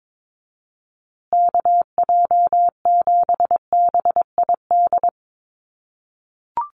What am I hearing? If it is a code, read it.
Morse code, heard as a keyed tone, about 22 wpm, 700 Hz. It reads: KJ7BID
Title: USA Call Signs Repeated 3x 22wpm